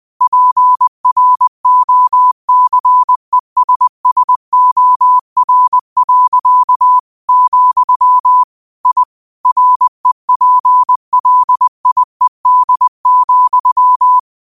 Morse code, heard as a keyed tone, about 20 wpm, 1000 Hz. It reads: PROCESSOR. , I REPLIED,
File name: qotdfiles / 20wpm